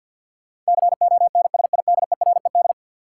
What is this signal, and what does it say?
Morse code, heard as a keyed tone, about 52 wpm, 700 Hz. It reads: CONSIDERED
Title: Sets of 5 Words - New Word Drill - 50wpm